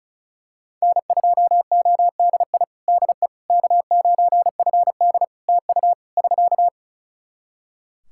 Morse code, heard as a keyed tone, about 34 wpm, 700 Hz. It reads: N2ODI DE K9FD TU <SK>
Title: Practice QSO - Exchange 4 - Ending - 35wpm